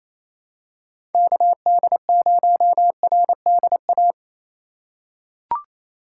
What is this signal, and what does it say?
Morse code, heard as a keyed tone, about 28 wpm, 700 Hz. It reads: KD0RDA
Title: USA Call Signs - 2x Character Spacing - 28wpm